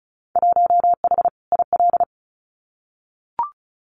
Morse code, heard as a keyed tone, about 34 wpm, 700 Hz. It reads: 1H IL
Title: Winter Field Day Exchanges 35wpm